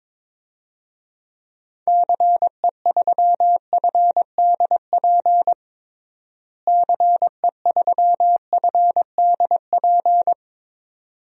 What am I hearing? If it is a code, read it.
Morse code, heard as a keyed tone, about 22 wpm, 700 Hz. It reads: CE3FDP CE3FDP
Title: DXCC-100 Call Signs - 6 Characters - Repeated 2x - 22wpm